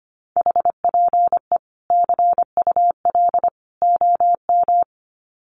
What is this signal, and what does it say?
Morse code, heard as a keyed tone, about 25 wpm, 700 Hz. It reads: HPE CUL OM
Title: Common QSO Phrases 25wpm